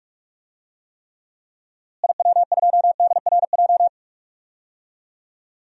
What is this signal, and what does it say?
Morse code, heard as a keyed tone, about 43 wpm, 700 Hz. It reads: IW2DRJ
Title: DXCC-100 Call Signs - 6 Characters - Repeated 3x - 45wpm